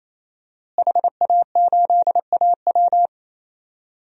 Morse code, heard as a keyed tone, about 28 wpm, 700 Hz. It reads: HA8AW
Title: DXCC-100 Call Signs - 5 Characters - Repeated 3x - 28wpm